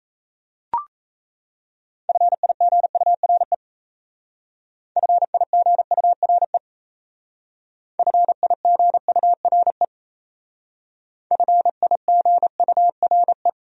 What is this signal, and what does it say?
Morse code, heard as a keyed tone, about 36 wpm, 700 Hz. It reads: FIGURE FIGURE FIGURE FIGURE